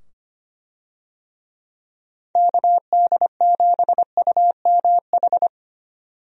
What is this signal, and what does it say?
Morse code, heard as a keyed tone, about 25 wpm, 700 Hz. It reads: KD7UMH
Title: USA Call Signs - 6 Characters - 25wpm